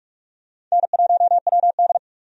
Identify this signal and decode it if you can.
Morse code, heard as a keyed tone, about 46 wpm, 700 Hz. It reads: N1WD
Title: USA Call Signs - 4 Characters - 45wpm